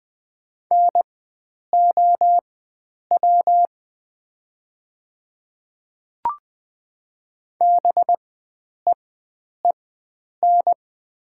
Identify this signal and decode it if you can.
Morse code, heard as a keyed tone, about 20 wpm, spaced as for 9 wpm, 700 Hz. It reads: NOW BEEN
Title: 4x Character Spacing - Top 100 Words - 20wpm